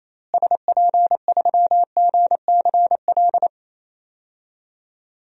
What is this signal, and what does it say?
Morse code, heard as a keyed tone, about 28 wpm, 700 Hz. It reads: SP3GCL
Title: DXCC-100 Call Signs - 6 Characters - Repeated 2x - 28wpm